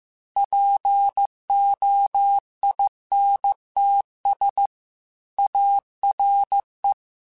Morse code, heard as a keyed tone, about 15 wpm, 800 Hz. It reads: POINTS ARE
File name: Technical Subjects / Technical Subjects-mp3s / Reactance @13